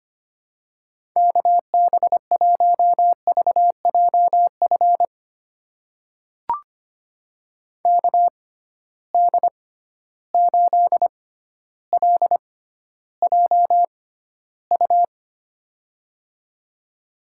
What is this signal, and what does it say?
Morse code, heard as a keyed tone, about 25 wpm, spaced as for 13 wpm, 700 Hz. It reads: KB1VJF KD8LJU
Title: USA Call Signs - 6 Characters - 6x Character Spacing - 25wpm